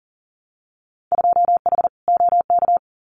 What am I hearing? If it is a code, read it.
Morse code, heard as a keyed tone, about 40 wpm, 700 Hz. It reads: 2H OK
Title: Winter Field Day Exchanges Encore 40wpm